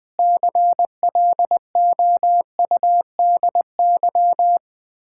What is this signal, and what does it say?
Morse code, heard as a keyed tone, about 20 wpm, 700 Hz. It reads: CLOUDY
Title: Common QSO Phrases - 3x Word-Spacing - 20wpm